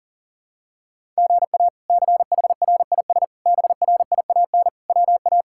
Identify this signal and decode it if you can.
Morse code, heard as a keyed tone, about 38 wpm, 700 Hz. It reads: GA CHRIS BRIAN WA
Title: K1USN SST QSO Practice - 40wpm